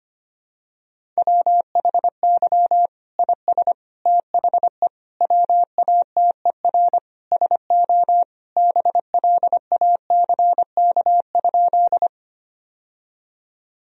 Morse code, heard as a keyed tone, about 25 wpm, 700 Hz. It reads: WHY IS THE WATER SO BLACK?
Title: Sentences from Top 1000 Words - Review Part 2 - 25wpm